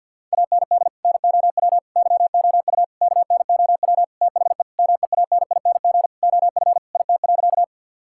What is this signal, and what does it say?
Morse code, heard as a keyed tone, about 50 wpm, 700 Hz. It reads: AND NOW YOU KNOW THE MEANING OF IT.
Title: Sentences from Top 1000 Words - Review Part 6 - 50wpm